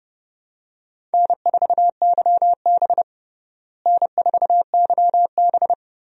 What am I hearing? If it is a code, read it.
Morse code, heard as a keyed tone, about 30 wpm, 700 Hz. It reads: N4YB N4YB
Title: USA Call Signs - 4 Characters - Repeated 3x - 30wpm